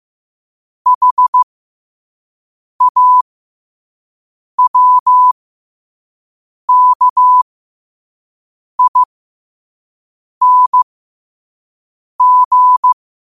Morse code, heard as a keyed tone, about 15 wpm, spaced as for 5 wpm, 1000 Hz. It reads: HAWKING
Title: 5 WPM morse code quotes for Fri, 01 Mar 2024